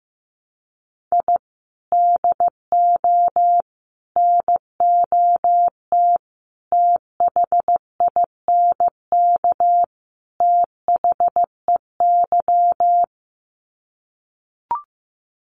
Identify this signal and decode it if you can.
Morse code, heard as a keyed tone, about 15 wpm, 700 Hz. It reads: I DO NOT THINK THEY
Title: Sets of 5 Words 15wpm